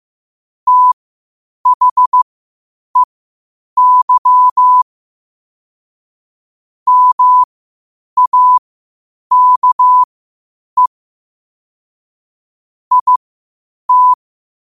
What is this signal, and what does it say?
Morse code, heard as a keyed tone, about 15 wpm, spaced as for 8 wpm, 1000 Hz. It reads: THEY MAKE IT
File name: qotdfiles / 8wpm